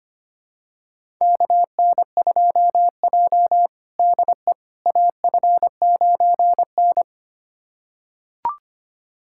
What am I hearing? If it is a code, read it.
Morse code, heard as a keyed tone, about 25 wpm, 700 Hz. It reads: KN2J DE AF9N
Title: USA Callsign DE Callsign - 4 Characters - 25wpm